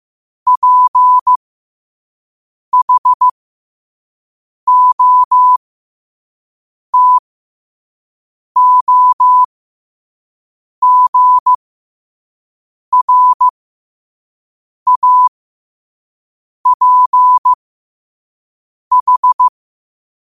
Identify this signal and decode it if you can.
Morse code, heard as a keyed tone, about 15 wpm, spaced as for 5 wpm, 1000 Hz. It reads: PHOTOGRAPH
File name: qotdfiles / 5wpm